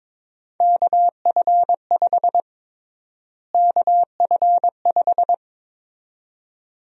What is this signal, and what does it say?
Morse code, heard as a keyed tone, about 22 wpm, 700 Hz. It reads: KF5 KF5
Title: USA Call Sign Prefixes - Repeated 2x - 22wpm